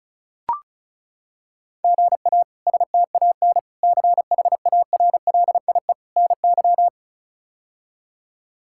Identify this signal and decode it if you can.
Morse code, heard as a keyed tone, about 36 wpm, 700 Hz. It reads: GA STAN CHARLIE NY
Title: K1USN SST QSO Practice - 35wpm